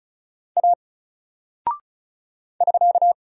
Morse code, heard as a keyed tone, about 36 wpm, 700 Hz. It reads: A <SK>
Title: Single Character (Letter:Number:Punctuation:Prosign) 35wpm